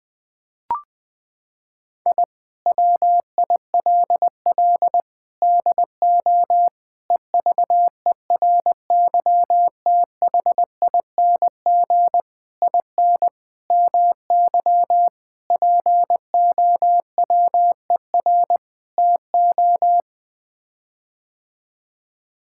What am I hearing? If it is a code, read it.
Morse code, heard as a keyed tone, about 21 wpm, 700 Hz. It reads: I WILL DO EVERYTHING IN MY POWER TO
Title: Sets of 8 Words - 20wpm